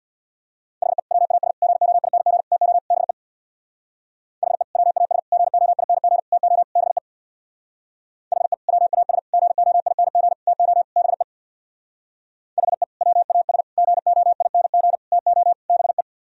Morse code, heard as a keyed tone, about 65 wpm, 700 Hz. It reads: HE WAS GOING TO BE HE WAS GOING TO BE HE WAS GOING TO BE HE WAS GOING TO BE